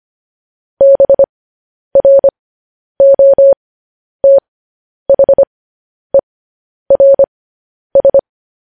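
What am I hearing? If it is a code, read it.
Morse code, heard as a keyed tone, about 25 wpm, spaced as for 10 wpm, 550 Hz. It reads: BROTHERS
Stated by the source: Morse code news headlines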